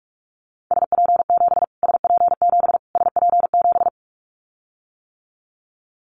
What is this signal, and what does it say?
Morse code, heard as a keyed tone, about 44 wpm, 700 Hz. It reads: SP7 SP7 SP7
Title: Callsign Prefixes for Top 25 DXCC - Repeated 3x - 45wpm